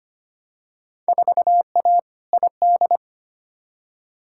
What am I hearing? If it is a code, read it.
Morse code, heard as a keyed tone, about 25 wpm, 700 Hz. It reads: 4A ID
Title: Field Day Exchanges - 25wpm